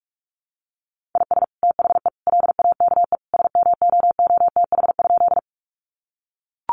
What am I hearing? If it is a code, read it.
Morse code, heard as a keyed tone, about 44 wpm, 700 Hz. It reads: IS THE LAKE SMOOTH?
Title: Sentences from Top 1400 Words 45wpm